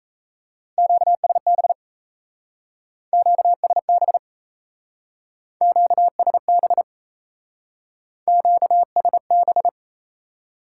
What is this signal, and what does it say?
Morse code, heard as a keyed tone, about 34 wpm, 700 Hz. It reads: QSB QSB QSB QSB
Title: Speed-Racing Top 180 QSO Elements 28wpm